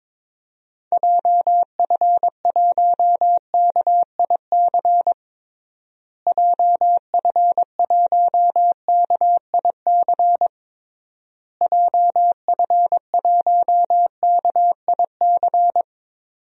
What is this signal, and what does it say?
Morse code, heard as a keyed tone, about 22 wpm, 700 Hz. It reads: JF1KIC JF1KIC JF1KIC
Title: DXCC-100 Call Signs - 6 Characters - Repeated 3x - 22wpm